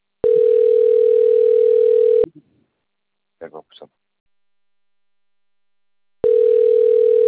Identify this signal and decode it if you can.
Ringback tone (North America), 440 + 480 Hz.